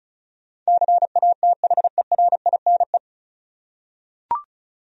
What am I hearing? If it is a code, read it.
Morse code, heard as a keyed tone, about 36 wpm, 700 Hz. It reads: CATHERINE
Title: USA Common Names - 35wpm